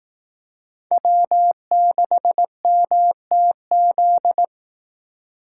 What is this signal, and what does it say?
Morse code, heard as a keyed tone, about 18 wpm, 700 Hz. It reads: W6MTZ
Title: Speed Racing - USA 1x3 Call Signs 15wpm